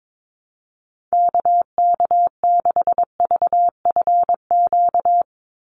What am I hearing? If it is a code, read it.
Morse code, heard as a keyed tone, about 22 wpm, 700 Hz. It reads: KK6VFQ
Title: USA Call Signs 22wpm